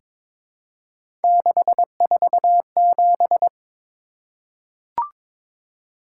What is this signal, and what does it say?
Morse code, heard as a keyed tone, about 23 wpm, 700 Hz. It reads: 647
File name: practice / 22wpm / Three Letter-Number Combinations - 22wpm